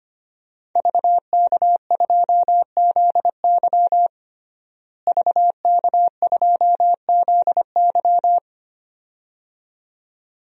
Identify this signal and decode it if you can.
Morse code, heard as a keyed tone, about 25 wpm, 700 Hz. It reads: VK2ZY VK2ZY
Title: DXCC-100 Call Signs - 5 Characters - Repeated 2x - 25wpm